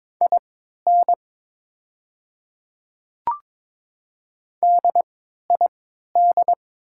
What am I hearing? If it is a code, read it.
Morse code, heard as a keyed tone, about 22 wpm, spaced as for 13 wpm, 700 Hz. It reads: IN DID
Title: 3x Character Spacing - Top 100 Words - 22wpm